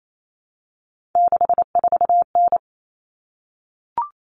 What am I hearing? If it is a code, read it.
Morse code, heard as a keyed tone, about 28 wpm, 700 Hz. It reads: 64N
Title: All Cut Numbers - 3-Digit 28wpm